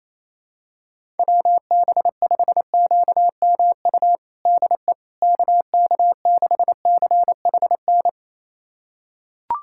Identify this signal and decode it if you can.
Morse code, heard as a keyed tone, about 28 wpm, 700 Hz. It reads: WB5QMU DE KK6CHN
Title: USA Callsign DE Callsign 28wpm